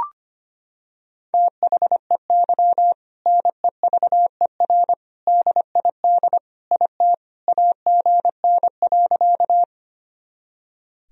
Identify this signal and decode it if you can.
Morse code, heard as a keyed tone, about 25 wpm, 700 Hz. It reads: THEY NEVER DID IT AGN.
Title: Sentences from Top 300 Words and Abbreviations 25wpm